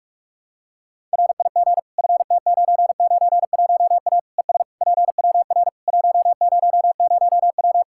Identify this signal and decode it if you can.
Morse code, heard as a keyed tone, about 46 wpm, 700 Hz. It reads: RIG FT991A ES PWR 100W
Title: Practice QSO - Exchange 2 - Info - 45wpm